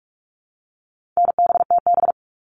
Morse code, heard as a keyed tone, about 44 wpm, 700 Hz. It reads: NBTB
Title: All Cut Numbers - 4-Digit 45wpm